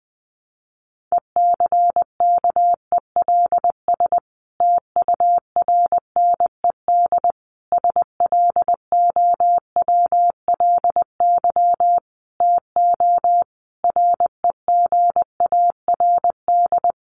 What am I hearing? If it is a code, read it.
Morse code, heard as a keyed tone, about 19 wpm, 700 Hz. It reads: ECKELS TURNED SLOWLY TO REGARD
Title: A Sound Of Thunder 20wpm